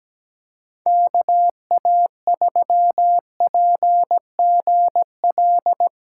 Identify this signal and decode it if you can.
Morse code, heard as a keyed tone, about 17 wpm, 700 Hz. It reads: KA3PGL
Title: USA Call Signs - 6 Characters - 6x Character Spacing - 17wpm